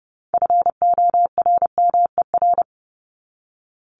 Morse code, heard as a keyed tone, about 30 wpm, 700 Hz. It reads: FORMER
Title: LY Suffix - Top 100 Words with Base Words 30wpm